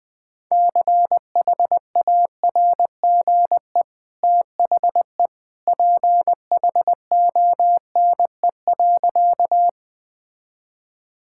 Morse code, heard as a keyed tone, about 21 wpm, 700 Hz. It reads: CHARGE THE PHONE.
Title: Easy Sentences with Hints - 20wpm